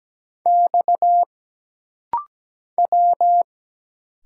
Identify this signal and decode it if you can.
Morse code, heard as a keyed tone, about 17 wpm, 700 Hz. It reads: X W
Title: Single Letters - Rapid-Fire 17wpm